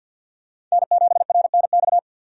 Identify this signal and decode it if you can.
Morse code, heard as a keyed tone, about 52 wpm, 700 Hz. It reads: N7RNX
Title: USA Call Signs - 5 Characters - 50wpm